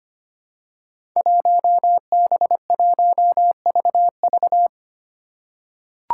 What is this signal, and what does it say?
Morse code, heard as a keyed tone, about 25 wpm, 700 Hz. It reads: 1B1VV
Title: Five Challenging Letter-Number Combinations 25wpm